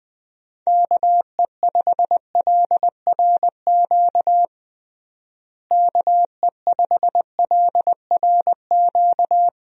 Morse code, heard as a keyed tone, about 21 wpm, 700 Hz. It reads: KE5LRQ KE5LRQ
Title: USA Call Signs - 6 Characters - Repeated 2x - 20wpm